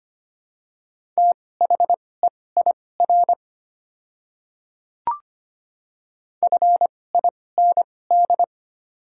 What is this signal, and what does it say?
Morse code, heard as a keyed tone, about 25 wpm, spaced as for 18 wpm, 700 Hz. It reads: THEIR FIND
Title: 2x Character Spacing - Top 100 Words - 25wpm